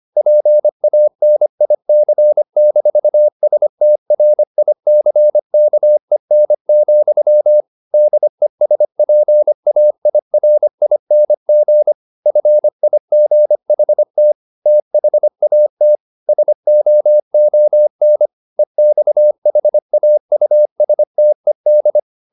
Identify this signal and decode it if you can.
Morse code, heard as a keyed tone, about 25 wpm, 600 Hz. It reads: PANIC-STRICKEN, DESPAIRING FIGHT THAT SOON EXHAUSTED
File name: the old man and the sea part 1 25wpm